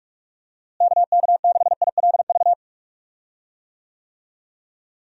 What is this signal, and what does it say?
Morse code, heard as a keyed tone, about 41 wpm, 700 Hz. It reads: KK6ILV
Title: USA Call Signs - 6 Characters - Repeated 2x - 45wpm